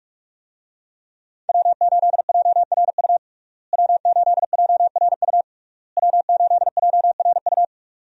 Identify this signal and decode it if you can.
Morse code, heard as a keyed tone, about 46 wpm, 700 Hz. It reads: W8JRU W8JRU W8JRU
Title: USA Call Signs - 5 Characters - Repeated 3x - 45wpm